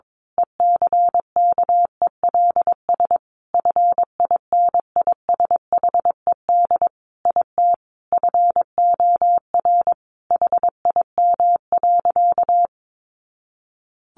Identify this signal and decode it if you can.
Morse code, heard as a keyed tone, about 22 wpm, 700 Hz. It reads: ECKELS FINISHED IT FOR HIM.
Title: A Sound Of Thunder 22wpm